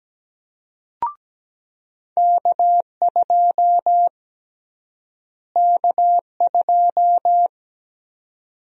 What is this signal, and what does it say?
Morse code, heard as a keyed tone, about 17 wpm, 700 Hz. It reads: K2 K2
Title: USA Call Sign Prefixes - Repeated 2x - 17wpm